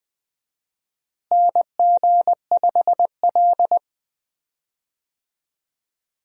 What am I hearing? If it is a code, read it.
Morse code, heard as a keyed tone, about 20 wpm, 700 Hz. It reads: NG5L
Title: USA Call Signs - 4 Characters - Repeated 2x - 20wpm